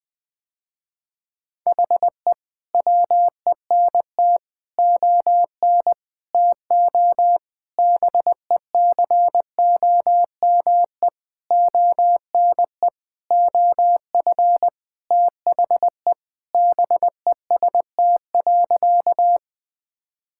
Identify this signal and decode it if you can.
Morse code, heard as a keyed tone, about 20 wpm, 700 Hz. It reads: HE WENT ON TO BECOME ONE OF THE BEST.
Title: Sentences with 7-Word Phrases - Spoken Beginning - 20wpm